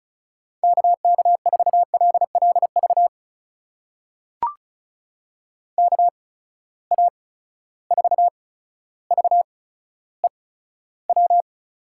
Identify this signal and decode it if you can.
Morse code, heard as a keyed tone, about 36 wpm, spaced as for 15 wpm, 700 Hz. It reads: KK4LLV KA4VEW
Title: USA Call Signs - 6 Characters - 8x Character Spacing - 35wpm